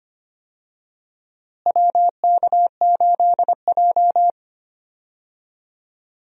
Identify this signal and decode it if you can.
Morse code, heard as a keyed tone, about 25 wpm, 700 Hz. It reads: WK8J
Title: USA Call Signs - 4 Characters - Repeated 2x - 25wpm